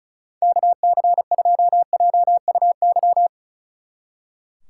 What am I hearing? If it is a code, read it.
Morse code, heard as a keyed tone, about 45 wpm, 700 Hz. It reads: KC2JUY